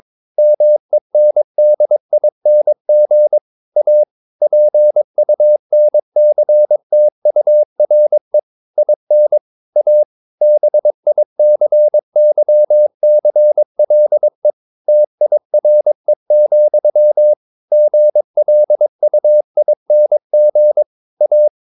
Morse code, heard as a keyed tone, about 22 wpm, 600 Hz. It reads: MENDING A PUNCTURE IN A BICYCLE TIRE, GLUING A
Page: Romanen Never af Ken Follet på engelsk.